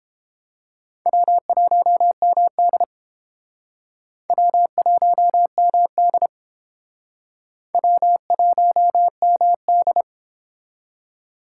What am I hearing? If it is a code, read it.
Morse code, heard as a keyed tone, about 29 wpm, 700 Hz. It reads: W1MD W1MD W1MD